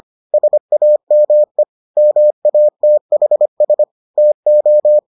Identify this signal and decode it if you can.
Morse code, heard as a keyed tone, about 25 wpm, 600 Hz. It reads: SAME MATHS TO